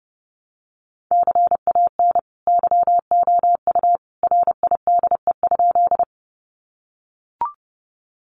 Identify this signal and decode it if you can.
Morse code, heard as a keyed tone, about 30 wpm, 700 Hz. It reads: CAN YOU RIDE?
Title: Sentences from Top 1200 Words 30wpm